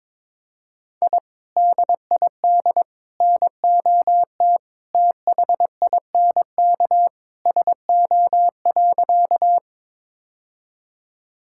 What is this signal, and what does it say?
Morse code, heard as a keyed tone, about 22 wpm, 700 Hz. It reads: I DID NOT THINK SO.
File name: practice / 22wpm / Sentences with 2-Word Phrases - 22wpm